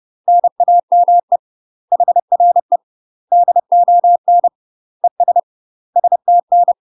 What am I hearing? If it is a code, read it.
Morse code, heard as a keyed tone, about 30 wpm, 700 Hz. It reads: NAME HRE DON ES STN
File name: UK MASTER QSOs TRAINING / 30WPM / UK MASTER QSOs 0018